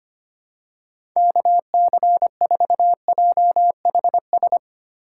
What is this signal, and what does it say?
Morse code, heard as a keyed tone, about 25 wpm, 700 Hz. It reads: KC4JHS
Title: USA Call Signs - 6 Characters - 6x Character Spacing - 25wpm